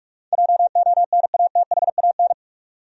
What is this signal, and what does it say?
Morse code, heard as a keyed tone, about 46 wpm, 700 Hz. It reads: JONATHAN
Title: USA Common Names - 45wpm